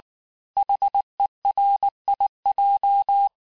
Morse code, heard as a keyed tone, about 19 wpm, 800 Hz. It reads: HERIJ
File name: cours-du-lundi files / cours-1583--texte-chiffre-18-mpm